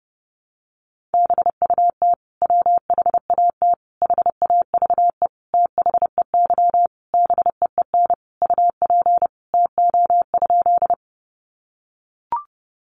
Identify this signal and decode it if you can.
Morse code, heard as a keyed tone, about 30 wpm, 700 Hz. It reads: BUT WHAT HAVE THEY BEEN UP TO?
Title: Sentences from Top 100 Words 30wpm